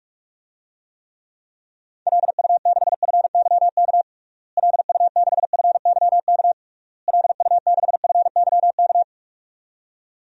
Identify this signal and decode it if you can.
Morse code, heard as a keyed tone, about 46 wpm, 700 Hz. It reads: LU6FYK LU6FYK LU6FYK
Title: DXCC-100 Call Signs - 6 Characters - Repeated 3x - 45wpm